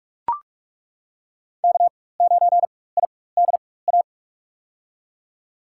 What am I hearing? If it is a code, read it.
Morse code, heard as a keyed tone, about 42 wpm, spaced as for 20 wpm, 700 Hz. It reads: K9IDA
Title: USA Call Signs - 5 Characters - 4x Character Spacing - 45wpm